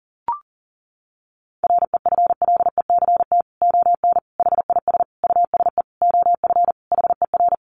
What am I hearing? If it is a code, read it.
Morse code, heard as a keyed tone, about 40 wpm, 700 Hz. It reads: REFLECT ON HIS USE OF HER
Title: Sentences from Top 1500 Words 40wpm